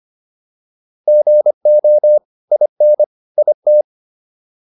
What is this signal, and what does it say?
Morse code, heard as a keyed tone, about 25 wpm, 600 Hz. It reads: GO IN IT